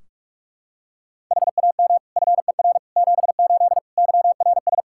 Morse code, heard as a keyed tone, about 47 wpm, 700 Hz. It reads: HAM FER 78 YRS
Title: Practice QSO - Exchange 3 - More Info - 45wpm